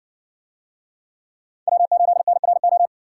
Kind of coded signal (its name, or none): Morse code